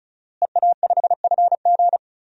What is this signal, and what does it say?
Morse code, heard as a keyed tone, about 34 wpm, 700 Hz. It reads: EA5FG
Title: DXCC-100 Call Signs - 5 Characters - Repeated 2x - 35wpm